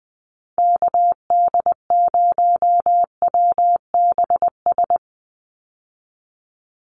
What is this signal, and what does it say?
Morse code, heard as a keyed tone, about 20 wpm, 700 Hz. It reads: KD0WBS
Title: USA Call Signs 20wpm